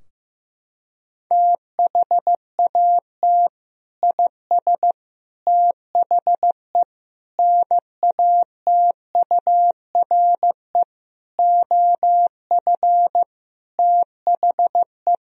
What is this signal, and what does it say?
Morse code, heard as a keyed tone, about 15 wpm, 700 Hz. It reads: THAT IS THE NATURE OF THE
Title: Sentences from Top 1000 Words - Review Part 4 - 15wpm